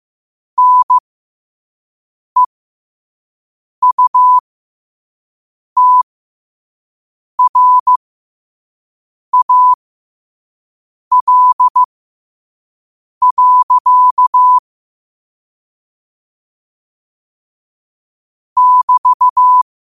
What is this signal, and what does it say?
Morse code, heard as a keyed tone, about 15 wpm, spaced as for 5 wpm, 1000 Hz. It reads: NEUTRAL. <BT>
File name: qotdfiles / 5wpm